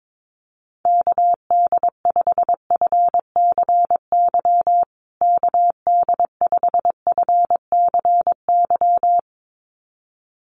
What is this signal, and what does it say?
Morse code, heard as a keyed tone, about 22 wpm, 700 Hz. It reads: KD5FCY KD5FCY
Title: USA Call Signs Repeated 2x 22wpm